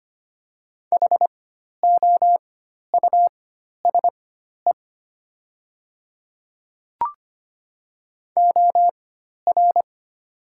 Morse code, heard as a keyed tone, about 25 wpm, spaced as for 12 wpm, 700 Hz. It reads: HOUSE OR
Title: 4x Character Spacing - Top 300 Words - 25wpm